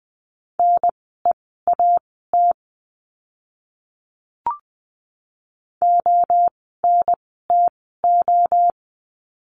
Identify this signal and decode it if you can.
Morse code, heard as a keyed tone, about 20 wpm, spaced as for 15 wpm, 700 Hz. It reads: NEAT ONTO
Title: 030 Lesson 6f - Letter i - Words 20wpm